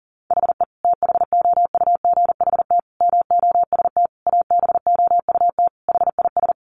Morse code, heard as a keyed tone, about 40 wpm, 700 Hz. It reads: HE THOUGHT MOST ABOUT HIS
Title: Sentences from Top 1500 Words 40wpm